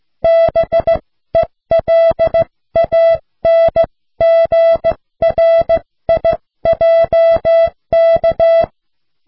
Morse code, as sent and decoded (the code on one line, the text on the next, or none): -... .-.. .- -. --. .-. .. .--- -.-
BELANGRIJK